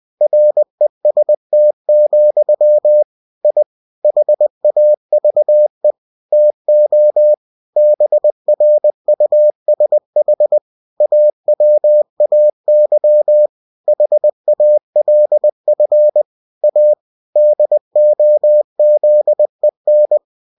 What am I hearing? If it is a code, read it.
Morse code, heard as a keyed tone, about 20 wpm, 600 Hz. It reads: REST, I HAVE TO BRUSH AWAY HALF A DOZEN
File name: cw-books / winnie-the-pooh / winnie-20wpm-page-0079